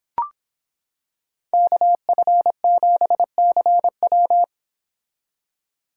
Morse code, heard as a keyed tone, about 25 wpm, 700 Hz. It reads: KF7CW